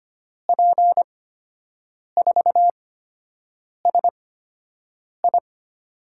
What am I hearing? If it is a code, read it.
Morse code, heard as a keyed tone, about 25 wpm, spaced as for 7 wpm, 700 Hz. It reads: P4SI
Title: USA Call Signs - 8x Character Spacing - 25wpm